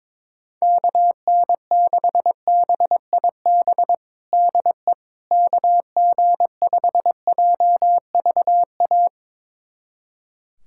Morse code, heard as a keyed tone, about 22 wpm, 700 Hz. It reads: KN6BIB DE KG5JVA
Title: USA Callsign DE Callsign - 6 Characters - 22wpm